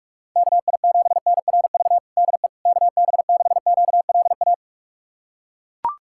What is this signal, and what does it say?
Morse code, heard as a keyed tone, about 42 wpm, 690 Hz. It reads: KI7NRV DE KB6QLA
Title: USA Callsign DE Callsign 45wpm